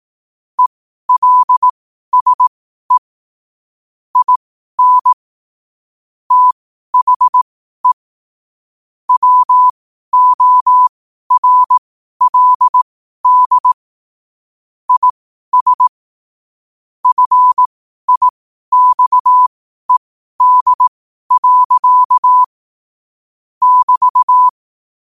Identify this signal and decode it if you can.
Morse code, heard as a keyed tone, about 18 wpm, spaced as for 12 wpm, 1000 Hz. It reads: ELSE IN THE WORLD IS FIXED. <BT>
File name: 12wpm